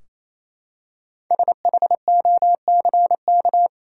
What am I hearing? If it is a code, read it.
Morse code, heard as a keyed tone, about 28 wpm, 700 Hz. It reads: SHOCK
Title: Five Letter Words - 28wpm